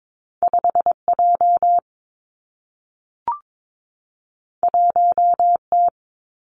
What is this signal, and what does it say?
Morse code, heard as a keyed tone, about 22 wpm, 700 Hz. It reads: 5 J 1 T
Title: Two Challenging Letter-Number Combinations 22wpm